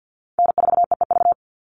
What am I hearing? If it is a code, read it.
Morse code, heard as a keyed tone, about 50 wpm, 700 Hz. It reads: N4EEV
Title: Call Signs - 50wpm